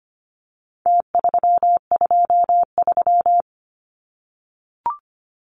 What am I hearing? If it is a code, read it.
Morse code, heard as a keyed tone, about 25 wpm, 700 Hz. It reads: T323
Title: Numbers - Contest Style - 4-Digit 25wpm